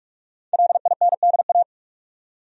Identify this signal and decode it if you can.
Morse code, heard as a keyed tone, about 42 wpm, 700 Hz. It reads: LINDA
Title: USA Common Names - 45wpm